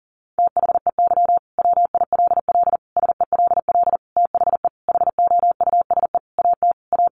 Morse code, heard as a keyed tone, about 40 wpm, 700 Hz. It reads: THEY WILL SELL THE HOUSE AT A
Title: Sentences from Top 1200 Words 40wpm